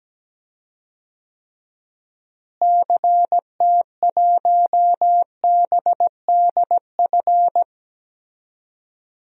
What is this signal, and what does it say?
Morse code, heard as a keyed tone, about 17 wpm, 700 Hz. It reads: CT1BDF
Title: DXCC-100 Call Signs - 6 Characters - Repeated 3x - 17wpm